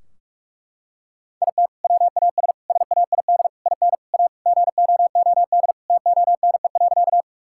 Morse code, heard as a keyed tone, about 46 wpm, 700 Hz. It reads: IT WAS SAID IN A GOOD TONE.
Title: Sentences from Top 1600 Words 45wpm